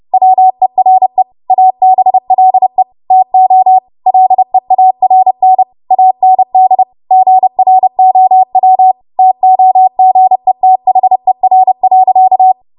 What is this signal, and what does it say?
Morse code, heard as a keyed tone, about 30 wpm, 750 Hz. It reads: WERE ABLE TO LEARN AND GROW TOGETHER.
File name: Morse / 30WPM